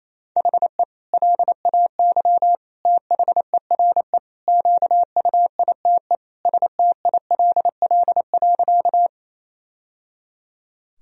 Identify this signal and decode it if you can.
Morse code, heard as a keyed tone, about 28 wpm, 700 Hz. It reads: HE LAY THERE QUITE STILL.
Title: Sentences from Top 1000 Words - Review Part 2 - 28wpm